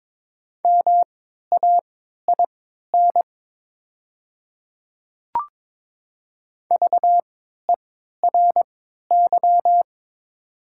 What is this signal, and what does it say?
Morse code, heard as a keyed tone, about 22 wpm, spaced as for 13 wpm, 700 Hz. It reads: MAIN VERY
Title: 3x Character Spacing - Top 300 Words - 22wpm